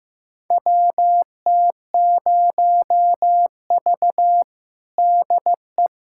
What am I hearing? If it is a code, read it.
Morse code, heard as a keyed tone, about 15 wpm, 700 Hz. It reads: WT0V DE
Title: USA Callsign DE Callsign - 4 Characters - 15wpm